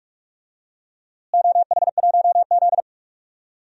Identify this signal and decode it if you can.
Morse code, heard as a keyed tone, about 46 wpm, 700 Hz. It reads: OH1Z
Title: DXCC-100 Call Signs - 4 Characters - Repeated 2x - 45wpm